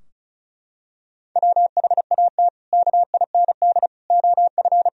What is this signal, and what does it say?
Morse code, heard as a keyed tone, about 36 wpm, 700 Hz. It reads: WHAT KIND OF
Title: Sentences from Top 1000 Words - Review Part 4 - 35wpm